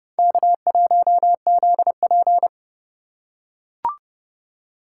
Morse code, heard as a keyed tone, about 32 wpm, 700 Hz. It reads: K1ZP
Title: USA Call Signs - 4 Characters - 8x Character Spacing - 30wpm